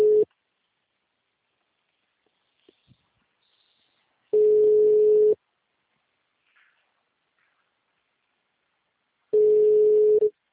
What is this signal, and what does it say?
Ringback tone (Europe, 425 Hz).
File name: out- -2048- - - .28693